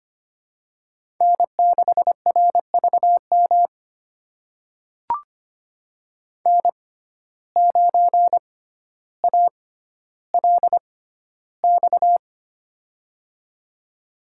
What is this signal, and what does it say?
Morse code, heard as a keyed tone, about 25 wpm, spaced as for 13 wpm, 700 Hz. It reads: N6RVM N9ALX
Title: USA Call Signs - 5 Characters - 6x Character Spacing - 25wpm